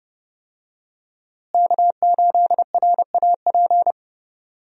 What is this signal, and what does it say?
Morse code, heard as a keyed tone, about 30 wpm, 700 Hz. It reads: K8RAP
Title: USA Call Signs - 5 Characters - Repeated 3x - 30wpm